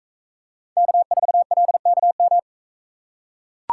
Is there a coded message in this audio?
Yes: Morse code from 0.8 to 2.4 s, heard as a keyed tone, about 42 wpm, 700 Hz. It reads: K4LKM